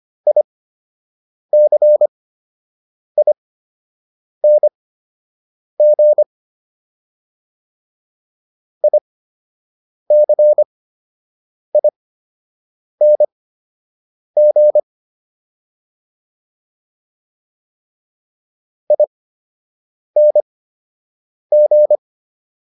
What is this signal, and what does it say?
Morse code, heard as a keyed tone, about 25 wpm, spaced as for 7 wpm, 600 Hz. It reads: ICING ICING ING